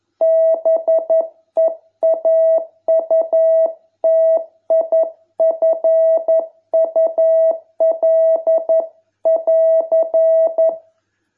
Morse code, heard as a keyed tone, about 11 wpm, 660 Hz. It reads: BEAUTIFUL<AR>